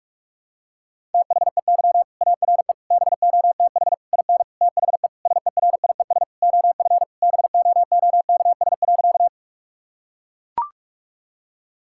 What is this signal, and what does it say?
Morse code, heard as a keyed tone, about 42 wpm, 700 Hz. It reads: THEY ARE BOTH IN THE SERIES OF BOOKS.
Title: Sentences from Top 1000 Words - Review Part 5 - 45wpm